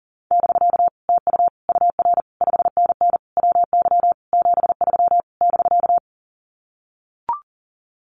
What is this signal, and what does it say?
Morse code, heard as a keyed tone, about 40 wpm, 700 Hz. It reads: <BK> TU UR 5NN WY 73 <BK>
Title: Common QSO Phrases Encore 40wpm